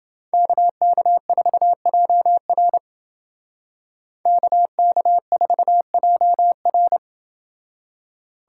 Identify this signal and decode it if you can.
Morse code, heard as a keyed tone, about 28 wpm, 700 Hz. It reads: KK4JR KK4JR